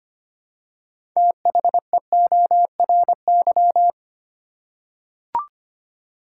Morse code, heard as a keyed tone, about 25 wpm, 700 Hz. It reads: THEORY
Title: Six Letter Words - 25wpm